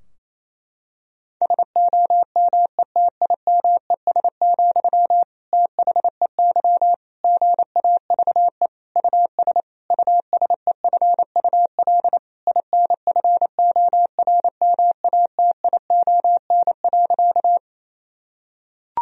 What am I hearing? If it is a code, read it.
Morse code, heard as a keyed tone, about 28 wpm, 700 Hz. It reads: SOMETIMES, THEY GAVE US USEFUL INFORMATION.
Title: Sentences from Top 1000 Words - Review Part 4 - 28wpm